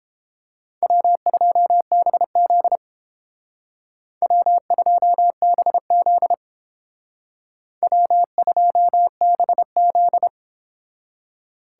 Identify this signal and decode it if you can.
Morse code, heard as a keyed tone, about 28 wpm, 700 Hz. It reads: W2BZ W2BZ W2BZ